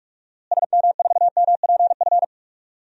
Morse code, heard as a keyed tone, about 43 wpm, 700 Hz. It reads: SM4MPF
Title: DXCC-100 Call Signs - 6 Characters - Repeated 2x - 45wpm